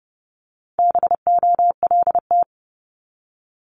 Morse code, heard as a keyed tone, about 30 wpm, 700 Hz. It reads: BOLT
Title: Multi-letter Phonemes 30wpm